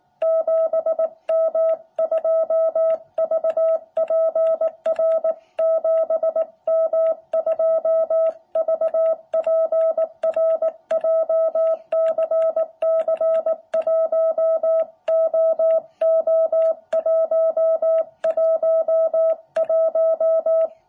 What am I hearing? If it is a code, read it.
Morse code, heard as a keyed tone, about 18 wpm, 660 Hz. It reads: 7M2VPR 7M2VPR JCC1OO111
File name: jh8cbh / 2021cw